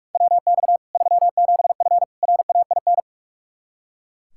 Morse code, heard as a keyed tone, about 41 wpm, 700 Hz. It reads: WX 37F RAIN
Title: Practice QSO - Exchange 2 - Info - 45wpm